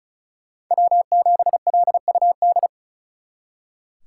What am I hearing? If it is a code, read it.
Morse code, heard as a keyed tone, about 47 wpm, spaced as for 41 wpm, 700 Hz. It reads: W7LUD
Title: Speed Racing - USA 1x3 Call Signs 35wpm